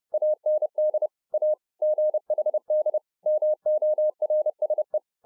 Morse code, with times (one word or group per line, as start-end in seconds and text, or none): 0.1-1.1 AND
1.3-1.5 A
1.8-3.0 GHD
3.3-5.0 MORSE